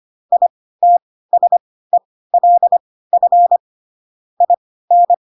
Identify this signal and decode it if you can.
Morse code, heard as a keyed tone, about 25 wpm, spaced as for 16 wpm, 700 Hz. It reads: ITSELF IN
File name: Pinocchio / Pinocchio ch1 10 25 ik1hge